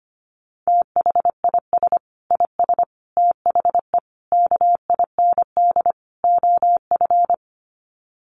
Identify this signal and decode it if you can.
Morse code, heard as a keyed tone, about 25 wpm, 700 Hz. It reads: THIS IS THE KIND OF
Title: Sets of 5 Words Encore 25wpm